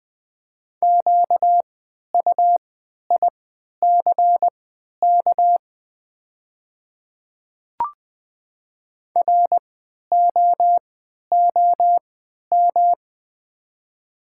Morse code, heard as a keyed tone, about 20 wpm, spaced as for 11 wpm, 700 Hz. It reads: QUICK ROOM
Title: 3x Character Spacing - Top 500 Words - 20wpm